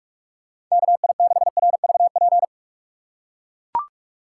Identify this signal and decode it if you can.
Morse code, heard as a keyed tone, about 42 wpm, 700 Hz. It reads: KI6RVP
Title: USA Call Signs - 6 Characters - 3x Character Spacing - 45wpm